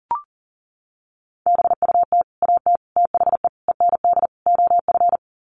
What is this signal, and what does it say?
Morse code, heard as a keyed tone, about 40 wpm, 700 Hz. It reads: BUT AT THE END OF
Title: Sets of 5 Words Encore2 40wpm